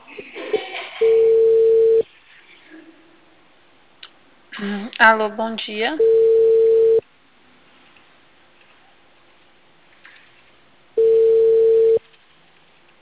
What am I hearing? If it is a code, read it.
Ringback tone (China, 450 Hz), with speech.